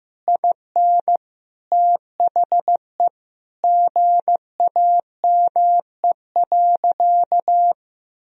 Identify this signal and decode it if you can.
Morse code, heard as a keyed tone, about 16 wpm, 700 Hz. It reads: IN THE GAME.
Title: Sentences from Top 1000 Words - Review Part 3 - 15wpm